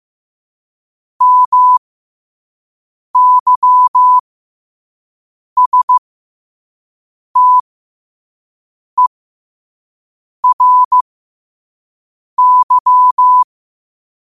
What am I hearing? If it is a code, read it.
Morse code, heard as a keyed tone, about 15 wpm, spaced as for 5 wpm, 1000 Hz. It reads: MYSTERY
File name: qotdfiles / 5wpm